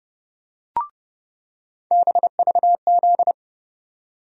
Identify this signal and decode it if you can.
Morse code, heard as a keyed tone, about 30 wpm, 700 Hz. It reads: BVZ
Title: USA Call Sign Postfixes - 3 Letters 30wpm